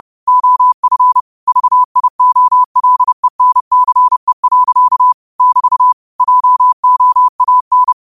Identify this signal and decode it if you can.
Morse code, heard as a keyed tone, about 30 wpm, 1000 Hz. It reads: OF VIOLENCE. <BT> JOAN
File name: 30wpm